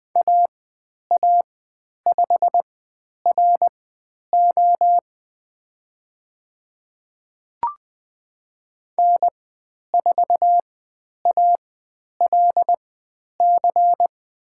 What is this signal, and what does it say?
Morse code, heard as a keyed tone, about 20 wpm, spaced as for 10 wpm, 700 Hz. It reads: AA5RO N4ALC
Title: 109 Lesson 18g - Letter m - Call Signs 20wpm